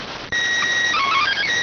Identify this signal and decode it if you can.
SSTV (slow-scan television), PD 290.